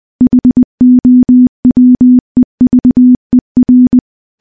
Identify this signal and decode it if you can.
Morse code, heard as a keyed tone, about 20 wpm, 250 Hz. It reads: HOWEVER
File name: morse-training-20WPM